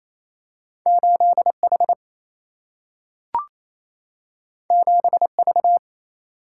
Morse code, heard as a keyed tone, about 28 wpm, 700 Hz. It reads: 8H 7V
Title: Single-Letter Single-Number 28wpm